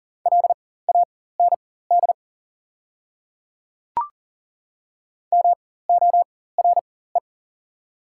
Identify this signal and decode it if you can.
Morse code, heard as a keyed tone, about 40 wpm, spaced as for 19 wpm, 690 Hz. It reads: LAND MORE
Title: 4x Character Spacing - Top 200 Words - 40wpm